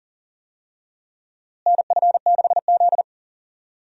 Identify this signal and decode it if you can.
Morse code, heard as a keyed tone, about 40 wpm, 700 Hz. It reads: NF6Z
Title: USA Call Signs - 4 Characters - Repeated 3x - 40wpm